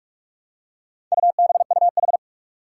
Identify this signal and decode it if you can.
Morse code, heard as a keyed tone, about 46 wpm, 700 Hz. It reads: UBUH
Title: Four Challenging Letter-Number Combinations 45wpm